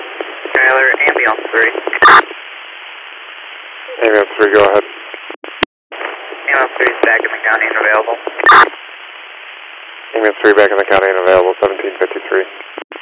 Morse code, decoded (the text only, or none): none